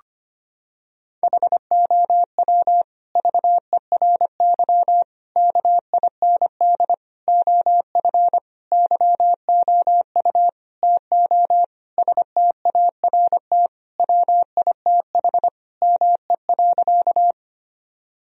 Morse code, heard as a keyed tone, about 25 wpm, 700 Hz. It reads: HOW VERY KIND OF YOU TO START WITH ME.
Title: Sentences from Top 1000 Words - Review Part 1 - 25wpm